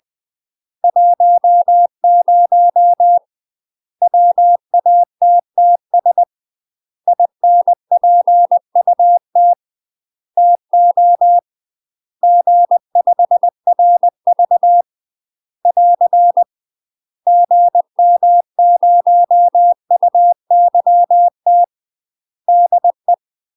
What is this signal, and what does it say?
Morse code, heard as a keyed tone, about 20 wpm, 700 Hz. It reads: 10 WATTS INPUT TO G5RV <AR> GM0UYT DE
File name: UK MASTER QSOs TRAINING / 20WPM / UK MASTER QSOs 0043